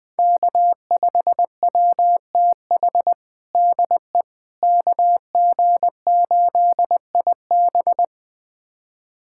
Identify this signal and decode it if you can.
Morse code, heard as a keyed tone, about 20 wpm, 700 Hz. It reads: K5WTH DE KG8IB
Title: USA Callsign DE Callsign - 5 Characters - 20wpm